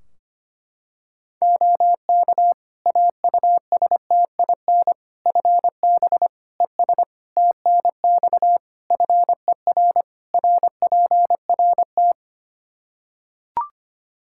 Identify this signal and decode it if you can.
Morse code, heard as a keyed tone, about 25 wpm, 700 Hz. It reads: OK AUSTIN FB ES TNX FER RPRT
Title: Practice QSO - Exchange 2 - Info - 25wpm